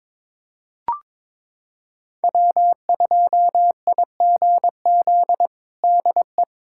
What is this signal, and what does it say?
Morse code, heard as a keyed tone, about 21 wpm, 700 Hz. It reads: W2IGZ DE
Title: USA Callsign DE Callsign - 5 Characters - 22wpm